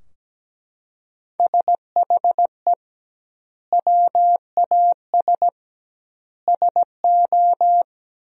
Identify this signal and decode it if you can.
Morse code, heard as a keyed tone, about 17 wpm, 700 Hz. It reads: SHE WAS SO
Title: 2x Word-Spacing - Sentences from Top 400 Words - 17wpm